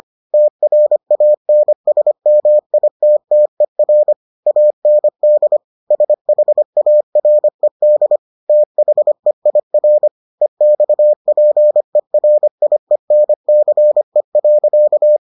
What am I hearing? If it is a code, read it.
Morse code, heard as a keyed tone, about 25 wpm, 600 Hz. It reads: TRANSMITTER AND SHARED THEIR EXPERIENCE.